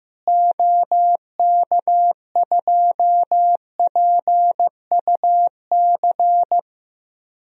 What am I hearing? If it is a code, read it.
Morse code, heard as a keyed tone, about 15 wpm, 700 Hz. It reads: OK2PUC
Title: DXCC-100 Call Signs - 6 Characters - Repeated 3x - 15wpm